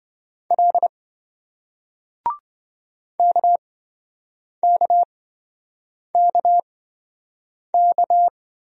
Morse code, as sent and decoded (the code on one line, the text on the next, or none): .-.. / -.- -.- -.- -.-
L KKKK